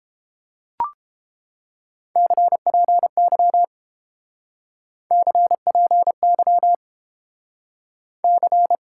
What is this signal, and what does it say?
Morse code, heard as a keyed tone, about 32 wpm, 700 Hz. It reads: CPY CPY C